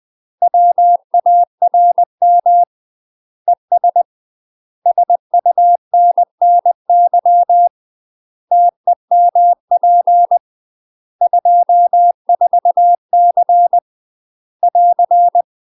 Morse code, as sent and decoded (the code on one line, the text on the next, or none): .-- .- .-. -- / . ... / ... ..- -. -. -.-- / - . -- .--. / ..--- ....- -.-. / .-.-.
WARM ES SUNNY TEMP 24C <AR>